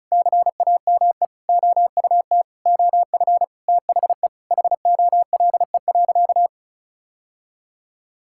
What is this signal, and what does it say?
Morse code, heard as a keyed tone, about 34 wpm, 700 Hz. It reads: CAME OUT OF THE HOLE.
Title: Sentences from Top 1600 Words 35wpm